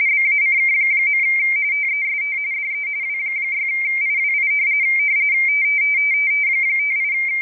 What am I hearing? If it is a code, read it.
RTTY, about 45.45 baud. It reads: BACK. TEST DE W6IWI 58
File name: rtty / websdr recording 14086.0kHz